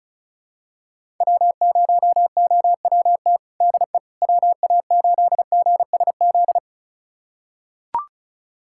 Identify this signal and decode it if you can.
Morse code, heard as a keyed tone, about 34 wpm, 700 Hz. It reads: W0OWT DE WA8GSZ
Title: USA Callsign DE Callsign 35wpm